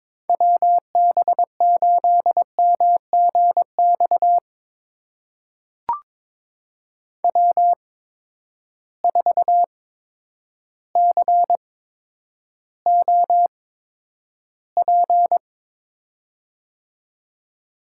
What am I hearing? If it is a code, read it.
Morse code, heard as a keyed tone, about 22 wpm, spaced as for 10 wpm, 700 Hz. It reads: WB8MGX W4COP
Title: USA Call Signs - 8x Character Spacing - 22wpm